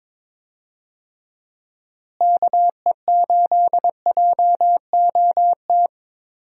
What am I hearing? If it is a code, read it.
Morse code, heard as a keyed tone, about 23 wpm, 700 Hz. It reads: KE8JOT
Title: USA Call Signs - 6 Characters - Repeated 3x - 22wpm